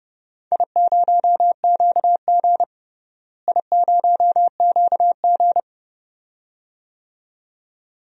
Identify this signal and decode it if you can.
Morse code, heard as a keyed tone, about 30 wpm, 700 Hz. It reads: I0QG I0QG
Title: DXCC-100 Call Signs - 4 Characters - Repeated 2x - 30wpm